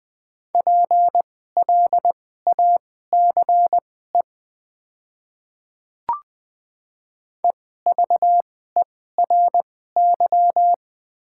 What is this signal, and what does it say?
Morse code, heard as a keyed tone, about 20 wpm, spaced as for 14 wpm, 700 Hz. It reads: PLACE EVERY
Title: 2x Character Spacing - Top 300 Words - 20wpm